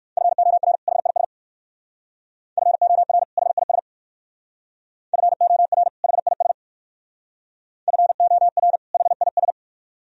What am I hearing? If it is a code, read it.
Morse code, heard as a keyed tone, about 56 wpm, 700 Hz. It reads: FOR HIS FOR HIS FOR HIS FOR HIS
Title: Speed-Racing - Sets of 2 Words 45wpm